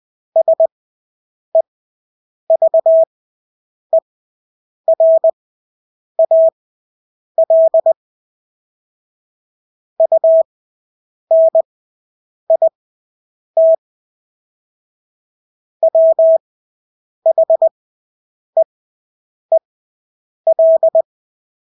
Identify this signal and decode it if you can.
Morse code, heard as a keyed tone, about 20 wpm, spaced as for 8 wpm, 650 Hz. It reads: SEVERAL UNIT WHEEL
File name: Engelske ord 20/8 wpm